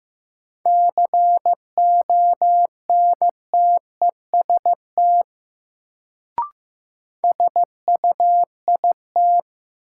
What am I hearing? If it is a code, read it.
Morse code, heard as a keyed tone, about 15 wpm, 700 Hz. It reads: CONTEST SUIT
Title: Sentences from Top 1600 Words - New Word Drill - 15wpm